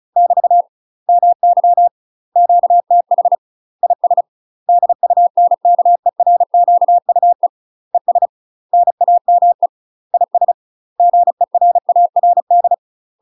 Morse code, heard as a keyed tone, about 33 wpm, 700 Hz. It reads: <BT> MY QTH IS DUNKERQUE ES NAME IS GERARD
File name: UK MASTER QSOs TRAINING / 35WPM / UK MASTER QSOs 0002